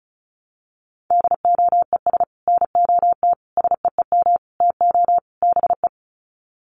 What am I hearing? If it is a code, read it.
Morse code, heard as a keyed tone, about 34 wpm, 700 Hz. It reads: DOES NOT SEEM TO BE
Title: Sets of 5 Words 35wpm